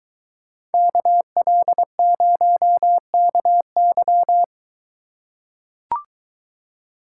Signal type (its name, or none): Morse code